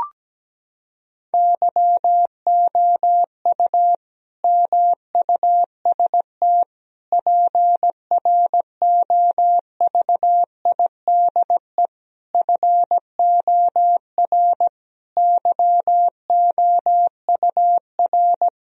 Morse code, heard as a keyed tone, about 17 wpm, 700 Hz. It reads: YOU MUST PROVIDE FOR YOUR
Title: Sentences from Top 1000 Words - Review Part 3 - 17wpm